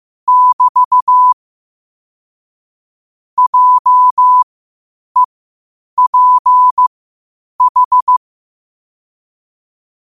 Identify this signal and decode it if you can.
Morse code, heard as a keyed tone, about 15 wpm, spaced as for 8 wpm, 1000 Hz. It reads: <BT> JEPH
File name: qotdfiles / 8wpm